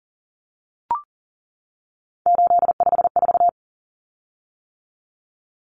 Morse code, heard as a keyed tone, about 40 wpm, 700 Hz. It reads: 854
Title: Common Cut Numbers - 3-Digit 40wpm